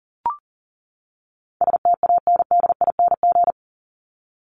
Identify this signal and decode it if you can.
Morse code, heard as a keyed tone, about 40 wpm, 700 Hz. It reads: STANDING
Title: ING Suffix - Top 100 Words with Base Words 40wpm